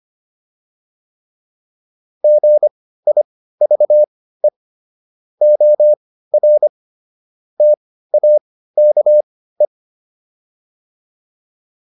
Morse code, heard as a keyed tone, about 25 wpm, spaced as for 15 wpm, 600 Hz. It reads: GIVE OR TAKE
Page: intermediate-practice-files